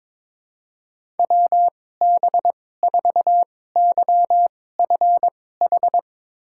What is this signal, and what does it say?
Morse code, heard as a keyed tone, about 22 wpm, spaced as for 16 wpm, 700 Hz. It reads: WB4YFH
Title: USA Call Signs - 2x Character Spacing - 22wpm